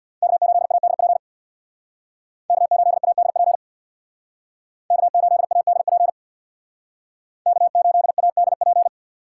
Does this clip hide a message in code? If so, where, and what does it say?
Yes: Morse code from 0.2 to 8.9 s, heard as a keyed tone, about 63 wpm, 700 Hz. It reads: K8ADP K8ADP K8ADP K8ADP